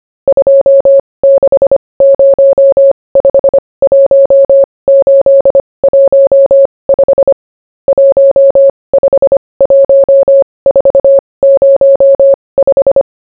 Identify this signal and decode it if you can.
Morse code, heard as a keyed tone, about 25 wpm, spaced as for 20 wpm, 550 Hz. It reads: 26051815 151405